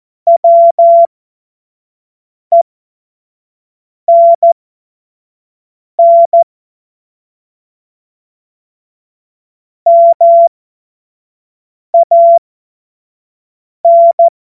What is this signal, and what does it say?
Morse code, heard as a keyed tone, about 14 wpm, spaced as for 5 wpm, 680 Hz. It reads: WENN MAN
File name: Morsekurs / Witze / Witz 13. Juli 2021, 25 F70